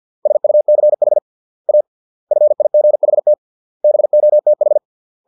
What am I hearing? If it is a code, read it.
Morse code, heard as a keyed tone, about 50 wpm, 600 Hz. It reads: SUCH A FIGHT BOTH